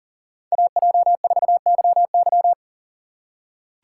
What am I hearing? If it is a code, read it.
Morse code, heard as a keyed tone, about 40 wpm, 700 Hz. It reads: AJ4YY
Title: USA Call Signs - 5 Characters - 6x Character Spacing - 40wpm